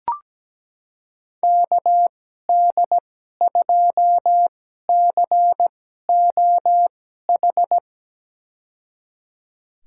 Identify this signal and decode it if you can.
Morse code, heard as a keyed tone, about 17 wpm, spaced as for 12 wpm, 700 Hz. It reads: KD2COH
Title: USA Call Signs - 6 Characters - 2x Character Spacing - 17wpm